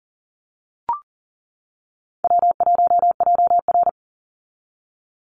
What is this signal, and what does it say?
Morse code, heard as a keyed tone, about 40 wpm, 700 Hz. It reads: W1JR
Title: Easy Call Signs - 40wpm